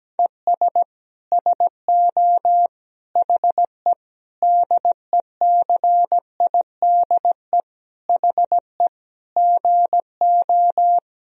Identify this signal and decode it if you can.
Morse code, heard as a keyed tone, about 17 wpm, 700 Hz. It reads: ES SO HE DECIDE HE GO
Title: Sentences from Top 500 Words and Abbreviations 17wpm